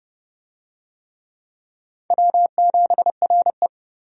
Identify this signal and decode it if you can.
Morse code, heard as a keyed tone, about 31 wpm, 700 Hz. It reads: W7RE
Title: USA Call Signs - 4 Characters - 30wpm